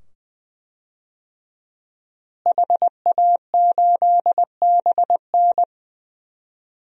Morse code, heard as a keyed tone, about 21 wpm, 700 Hz. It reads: HA8BN
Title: DXCC-100 Call Signs - 5 Characters - Repeated 2x - 20wpm